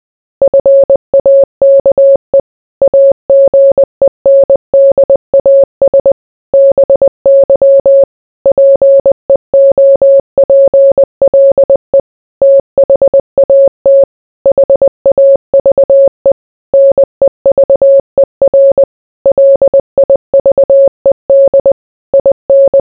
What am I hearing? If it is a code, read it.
Morse code, heard as a keyed tone, about 20 wpm, 550 Hz. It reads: FAKE AGENDAS BY PEOPLE THAT HAVE NEVER LIVED IN